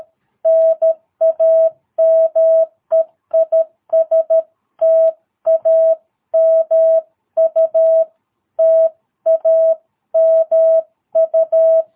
Morse code, heard as a keyed tone, about 13 wpm, 660 Hz. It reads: NAME IS TAMU TAMU
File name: jh8cbh / 2022cw